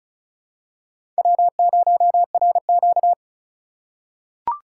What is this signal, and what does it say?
Morse code, heard as a keyed tone, about 34 wpm, 700 Hz. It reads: W0RQ
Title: USA Call Signs - 4 Characters - 6x Character Spacing - 35wpm